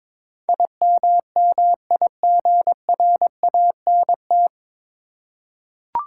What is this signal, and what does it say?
Morse code, heard as a keyed tone, about 21 wpm, 700 Hz. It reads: IMMIGRANT
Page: Morse Code Ninja practice 22wpm